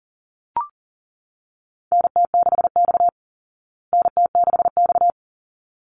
Morse code, heard as a keyed tone, about 40 wpm, 700 Hz. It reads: NT6X NT6X
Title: Call Signs Repeated 3x - 40wpm